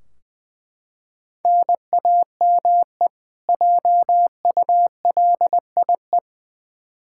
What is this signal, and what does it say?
Morse code, heard as a keyed tone, about 20 wpm, 700 Hz. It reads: NAME JULIE
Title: K1USN SST QSO Practice 20wpm